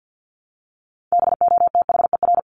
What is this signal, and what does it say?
Morse code, heard as a keyed tone, about 48 wpm, 700 Hz. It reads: BOTHER
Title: Top 1501-1750 Words 50wpm